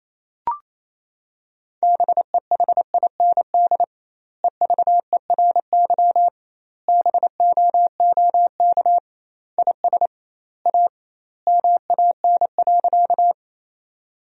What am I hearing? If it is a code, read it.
Morse code, heard as a keyed tone, about 27 wpm, 700 Hz. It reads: BEHIND EVERY BOOK IS A MAN.
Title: 2x Word-Spacing - Sentences from Top 500 Words - 28wpm